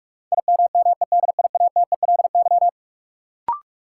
Morse code, heard as a keyed tone, about 44 wpm, 700 Hz. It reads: IMMEDIATELY
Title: Sentences from Top 1100 Words - New Word Drill - 45wpm